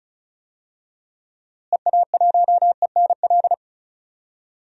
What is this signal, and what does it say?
Morse code, heard as a keyed tone, about 36 wpm, 700 Hz. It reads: EA1ENL
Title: DXCC-100 Call Signs - 6 Characters - Repeated 3x - 35wpm